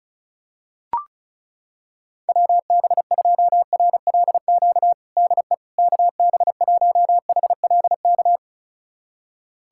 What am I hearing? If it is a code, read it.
Morse code, heard as a keyed tone, about 36 wpm, 700 Hz. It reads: WB2RLQ DE KB1HLK
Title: USA Callsign DE Callsign - 6 Characters - 35wpm